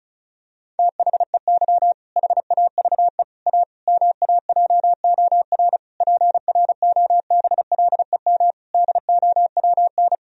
Morse code, heard as a keyed tone, about 36 wpm, 700 Hz. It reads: THEY HAVE A MAJOR PROBLEM DOWN
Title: Sentences from Top 1000 Words - Review Part 4 - 35wpm